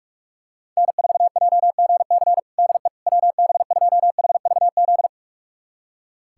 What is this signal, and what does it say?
Morse code, heard as a keyed tone, about 42 wpm, 700 Hz. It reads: N4JGC DE WB2HVZ
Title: USA Callsign DE Callsign 45wpm